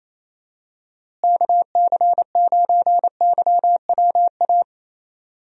Morse code, heard as a keyed tone, about 28 wpm, 700 Hz. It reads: KC9YWA
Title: USA Call Signs - 6 Characters - Repeated 2x - 28wpm